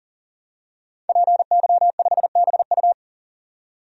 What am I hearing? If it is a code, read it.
Morse code, heard as a keyed tone, about 40 wpm, 700 Hz. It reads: PY5BU
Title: DXCC-100 Call Signs - 5 Characters - 40wpm